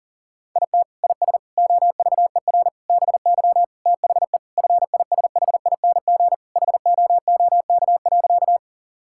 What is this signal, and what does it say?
Morse code, heard as a keyed tone, about 40 wpm, 700 Hz. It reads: IT IS OVER BY THE FISHING HOOK.
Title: Sentences from Top 1600 Words 40wpm